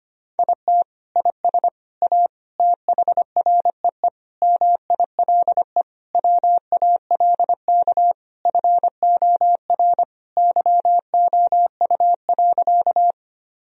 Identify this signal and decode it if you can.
Morse code, heard as a keyed tone, about 25 wpm, 700 Hz. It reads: IT IS A THREE MILE WALK FOR YOU.
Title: Sentences from Top 1000 Words - Review Part 1 - 25wpm